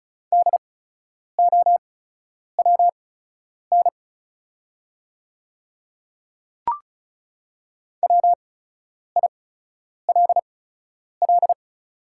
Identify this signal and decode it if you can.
Morse code, heard as a keyed tone, about 36 wpm, spaced as for 10 wpm, 700 Hz. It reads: DOWN WILL
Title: 8x Character Spacing - Top 100 Words - 35wpm